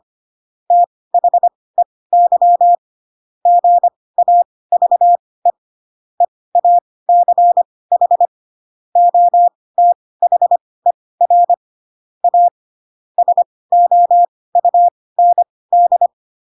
Morse code, heard as a keyed tone, about 25 wpm, spaced as for 18 wpm, 700 Hz. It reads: THEY GAVE EACH OTHER A SOUND
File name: Pinocchio / Pinocchio ch2 12 25 ik1hge